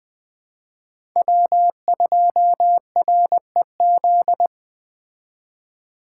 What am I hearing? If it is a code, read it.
Morse code, heard as a keyed tone, about 20 wpm, 700 Hz. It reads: W2REZ